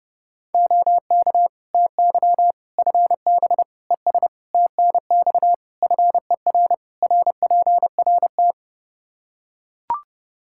Morse code, heard as a keyed tone, about 30 wpm, 700 Hz. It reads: OK TY FB ES TNX FER RPRT
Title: K1USN SST QSO Practice 30wpm